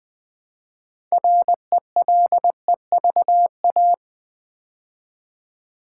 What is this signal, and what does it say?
Morse code, heard as a keyed tone, about 20 wpm, 700 Hz. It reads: RELEVA
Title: Build A Word - 3 Letters At A Time - 9-Letter Words - 20wpm